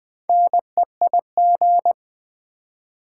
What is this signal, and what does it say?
Morse code, heard as a keyed tone, about 20 wpm, 700 Hz. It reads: NEIG
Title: Build A Word - 4 Letters At A Time - 11-Letter Words - 20wpm